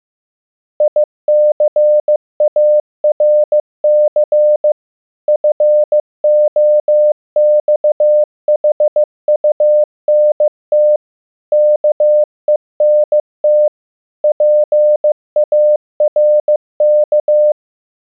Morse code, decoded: ICARC FOXHUNT KENT PARK